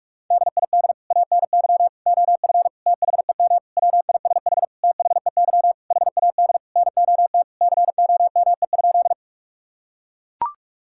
Morse code, heard as a keyed tone, about 41 wpm, 700 Hz. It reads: DID ANY OF THEM WISH THEY HAD NOT COME?
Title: Sentences from Top 1600 Words 45wpm